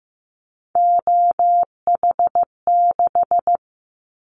Morse code, heard as a keyed tone, about 15 wpm, 700 Hz. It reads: OH6
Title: Callsign Prefixes for Top 25 DXCC - 15wpm